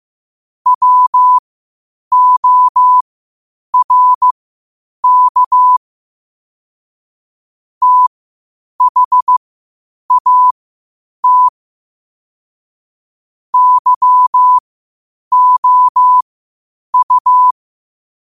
Morse code, heard as a keyed tone, about 15 wpm, spaced as for 8 wpm, 1000 Hz. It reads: WORK THAT YOU
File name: qotdfiles / 8wpm